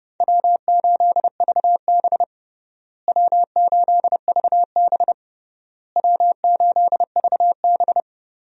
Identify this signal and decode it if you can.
Morse code, heard as a keyed tone, about 30 wpm, 700 Hz. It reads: W8VB W8VB W8VB
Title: USA Call Signs - 4 Characters - Repeated 3x - 30wpm